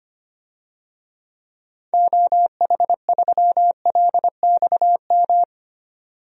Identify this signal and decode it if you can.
Morse code, heard as a keyed tone, about 25 wpm, 700 Hz. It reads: OH3LXM
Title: DXCC-100 Call Signs - 6 Characters - 25wpm